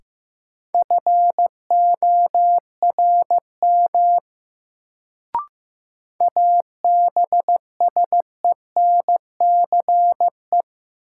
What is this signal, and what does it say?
Morse code, heard as a keyed tone, about 15 wpm, 700 Hz. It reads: FORM ABSENCE
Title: Sets of 4 Words - New Word Drill - 15wpm